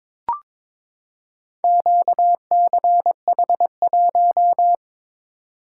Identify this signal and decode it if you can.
Morse code, heard as a keyed tone, about 22 wpm, 700 Hz. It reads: QCH1
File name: practice / 22wpm / Four Letter-Number Combinations - 22wpm